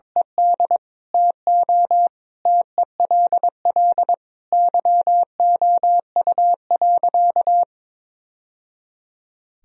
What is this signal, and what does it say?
Morse code, heard as a keyed tone, about 22 wpm, 700 Hz. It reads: ED TO TELL YOU.
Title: Sentences from Top 1000 Words - Review Part 1 - 22wpm